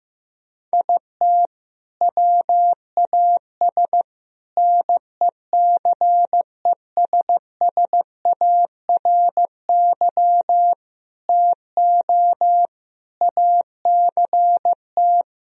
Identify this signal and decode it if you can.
Morse code, heard as a keyed tone, about 15 wpm, 700 Hz. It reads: IT WAS NECESSARY TO ACT
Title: Sentences from Top 1000 Words - Review Part 1 - 15wpm